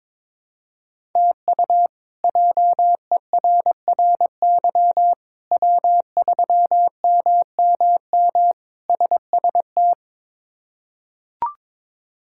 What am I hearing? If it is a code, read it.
Morse code, heard as a keyed tone, about 22 wpm, 700 Hz. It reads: TU JERRY W3MMM SST
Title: K1USN SST QSO Practice - 22wpm